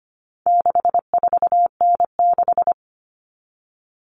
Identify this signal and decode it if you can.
Morse code, heard as a keyed tone, about 25 wpm, 700 Hz. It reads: 64N6
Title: All Cut Numbers - 4-Digit 25wpm